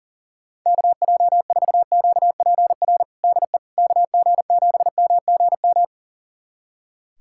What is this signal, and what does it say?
Morse code, heard as a keyed tone, about 40 wpm, 700 Hz. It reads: KJ4QPR DE KG7MGM
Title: USA Callsign DE Callsign 40wpm